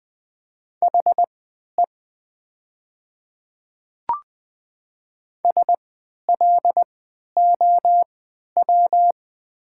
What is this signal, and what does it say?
Morse code, heard as a keyed tone, about 20 wpm, spaced as for 11 wpm, 700 Hz. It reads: HE SLOW
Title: 3x Character Spacing - Top 500 Words - 20wpm